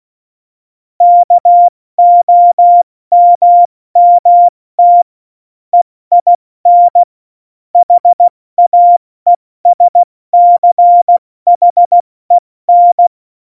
Morse code, heard as a keyed tone, about 16 wpm, 700 Hz. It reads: KOMMT EIN HAESCHEN
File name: Morsekurs / Witze / Witz 14 April 2020, 70 F80